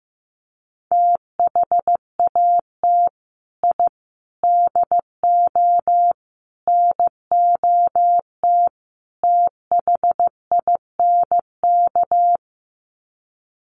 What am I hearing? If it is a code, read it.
Morse code, heard as a keyed tone, about 15 wpm, 700 Hz. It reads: THAT I DO NOT THINK
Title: Sets of 5 Words Encore 15wpm